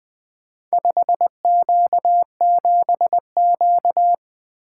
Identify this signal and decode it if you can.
Morse code, heard as a keyed tone, about 23 wpm, 700 Hz. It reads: 5Q7Q
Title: Speed-Racing - Four Letter-Number Combinations - 20wpm